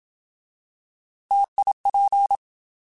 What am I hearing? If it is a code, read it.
Morse code, heard as a keyed tone, about 27 wpm, 800 Hz. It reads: TIP
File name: Alphabetical / Alphabetical mp3s / Ts @25